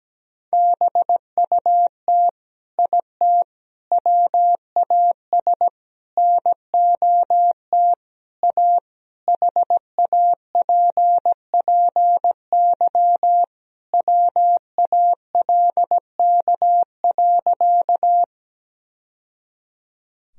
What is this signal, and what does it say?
Morse code, heard as a keyed tone, about 17 wpm, 700 Hz. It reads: BUT IT WAS NOT A HAPPY WALK.
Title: Sentences from Top 1000 Words - Review Part 1 - 17wpm